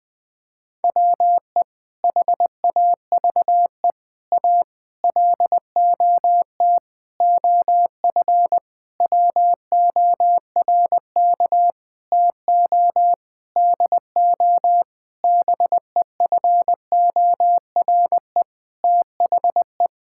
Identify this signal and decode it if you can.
Morse code, heard as a keyed tone, about 20 wpm, 700 Hz. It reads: WE HAVE A LOT OF WORK TO DO BEFORE THE
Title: Sentences with 8-Word Phrases - 20wpm